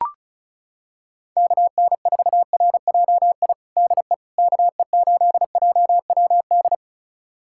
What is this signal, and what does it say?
Morse code, heard as a keyed tone, about 36 wpm, 700 Hz. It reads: KN4RJI DE KE8JWD
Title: USA Callsign DE Callsign - 6 Characters - 35wpm